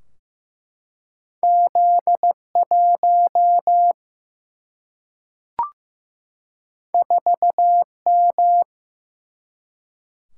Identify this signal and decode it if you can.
Morse code, heard as a keyed tone, about 15 wpm, 700 Hz. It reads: Z1 4M
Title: Single-Letter Single-Number 15wpm